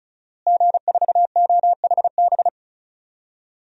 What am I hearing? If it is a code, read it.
Morse code, heard as a keyed tone, about 36 wpm, 700 Hz. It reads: G4OHB
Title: DXCC-100 Call Signs - 5 Characters - Repeated 2x - 35wpm